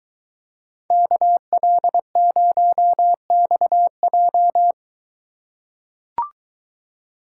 Morse code, heard as a keyed tone, about 22 wpm, 700 Hz. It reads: KL0XJ